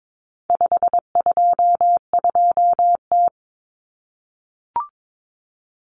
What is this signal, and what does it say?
Morse code, heard as a keyed tone, about 22 wpm, 700 Hz. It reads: 522T
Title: Common Cut Numbers - 4-Digit 22wpm